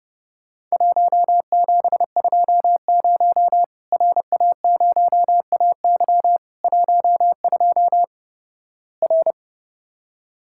Morse code, heard as a keyed tone, about 32 wpm, 700 Hz. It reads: 1720 RA0AY 12 R
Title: Worked All Europe DX Contest - QTC Exchange - 30wpm